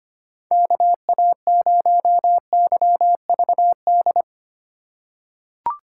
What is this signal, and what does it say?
Morse code, heard as a keyed tone, about 25 wpm, 700 Hz. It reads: KA0YVD
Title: USA Call Signs - 3x Character Spacing - 25wpm